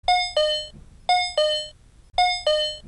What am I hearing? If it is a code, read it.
There is no code.